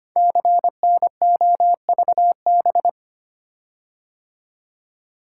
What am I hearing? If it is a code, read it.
Morse code, heard as a keyed tone, about 25 wpm, 700 Hz. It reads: CNOVB
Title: Code Groups - 5 Letters - 25wpm